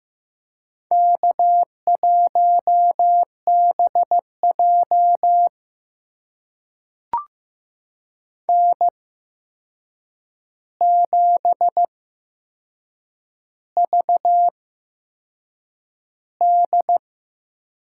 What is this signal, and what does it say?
Morse code, heard as a keyed tone, about 15 wpm, spaced as for 6 wpm, 700 Hz. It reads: K1BJ N7VD
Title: USA Call Signs - 4 Characters - 8x Character Spacing - 15wpm